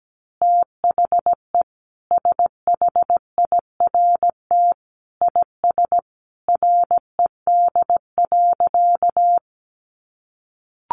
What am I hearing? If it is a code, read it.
Morse code, heard as a keyed tone, about 17 wpm, 700 Hz. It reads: THE SHIRT IS RED.
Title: Additional Easy Phrases 17wpm